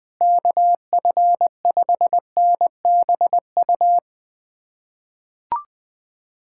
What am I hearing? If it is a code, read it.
Morse code, heard as a keyed tone, about 20 wpm, 700 Hz. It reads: KF5NBU
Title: USA Callsign DE Callsign - 6 Characters - 20wpm